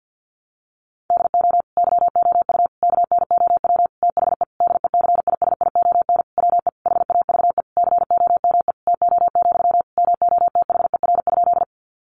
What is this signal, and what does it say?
Morse code, heard as a keyed tone, about 50 wpm, 700 Hz. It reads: DO YOU KNOW THE DECISION WE HAVE COME TO, MOTHER?
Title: Sentences from Top 1000 Words 50wpm